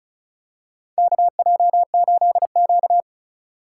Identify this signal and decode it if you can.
Morse code, heard as a keyed tone, about 36 wpm, 700 Hz. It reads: KJ8Q
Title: USA Call Signs - 3x Character Spacing - 35wpm